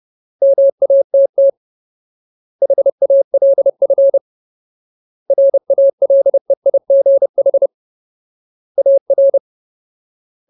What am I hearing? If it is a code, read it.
Morse code, heard as a keyed tone, about 30 wpm, 550 Hz. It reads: MATT HALF RALEIGH AR
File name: Ultimate Word List / 30WPM Ultimate